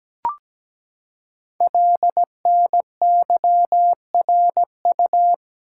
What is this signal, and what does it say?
Morse code, heard as a keyed tone, about 17 wpm, 700 Hz. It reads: LNYRU
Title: Code Groups - 5 Letter:Numbers - 17wpm